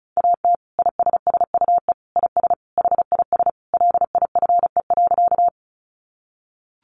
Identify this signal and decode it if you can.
Morse code, heard as a keyed tone, about 34 wpm, 700 Hz. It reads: AT ISSUE IS HIS LIFE.
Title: Sentences from Top 900 Words 35wpm